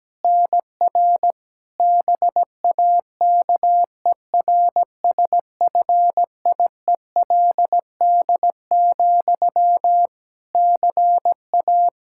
Morse code, heard as a keyed tone, about 17 wpm, 700 Hz. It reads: NR BAKERSFIELD, CA
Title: Practice QSO - Exchange 1 - Beginning - 17wpm